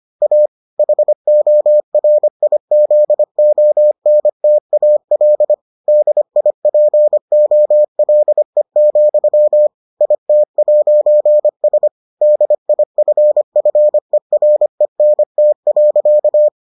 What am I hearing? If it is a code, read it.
Morse code, heard as a keyed tone, about 25 wpm, 600 Hz. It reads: A HORIZONTAL DIPOLE, IT'S DIFFERENT.